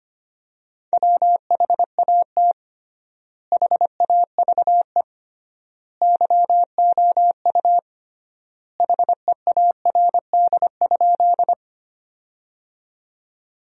Morse code, heard as a keyed tone, about 25 wpm, 700 Hz. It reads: WHAT HAVE YOU HEARD?
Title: 3x Word-Spacing - Sentences from Top 400 Words - 25wpm